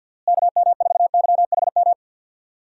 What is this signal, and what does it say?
Morse code, heard as a keyed tone, about 47 wpm, 700 Hz. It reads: KM4YHM
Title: USA Call Signs - 6 Characters - Repeated 3x - 50wpm